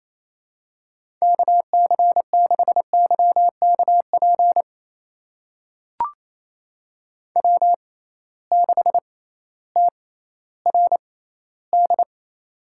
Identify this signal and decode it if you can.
Morse code, heard as a keyed tone, about 28 wpm, spaced as for 15 wpm, 700 Hz. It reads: KC6YKP W6TRD
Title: USA Call Signs - 6x Character Spacing - 28wpm